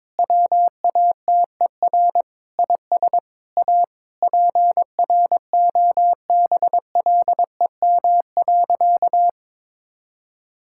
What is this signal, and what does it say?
Morse code, heard as a keyed tone, about 21 wpm, 700 Hz. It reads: WATER IS A PROBLEM.
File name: practice / 22wpm / Sentences with 3-Word Phrases - 22wpm